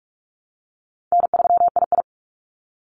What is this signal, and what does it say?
Morse code, heard as a keyed tone, about 44 wpm, 700 Hz. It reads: N3II
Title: NA SOTA Activator Callsigns 45wpm